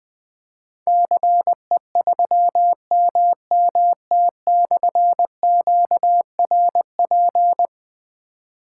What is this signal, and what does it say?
Morse code, heard as a keyed tone, about 20 wpm, 700 Hz. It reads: CE3MMT/QRP
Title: Difficult Call Signs 20wpm